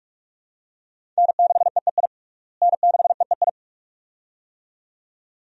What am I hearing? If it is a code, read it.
Morse code, heard as a keyed tone, about 41 wpm, 700 Hz. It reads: N6EEI N6EEI
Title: USA Call Signs - 5 Characters - Repeated 2x - 45wpm